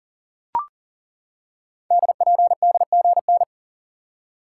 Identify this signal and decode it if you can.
Morse code, heard as a keyed tone, about 39 wpm, 700 Hz. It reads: DPDGN
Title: Code Groups - 5 Letters - 40wpm